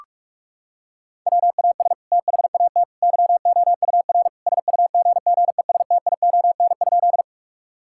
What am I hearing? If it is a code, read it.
Morse code, heard as a keyed tone, about 46 wpm, 700 Hz. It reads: WAS THAT YOUR SUGGESTION?
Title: Sentences from Top 1600 Words 45wpm